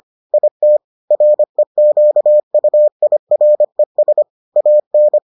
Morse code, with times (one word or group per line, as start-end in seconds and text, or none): 0.3-0.8 IT
1.1-4.2 REQUIRES
4.6-5.2 AN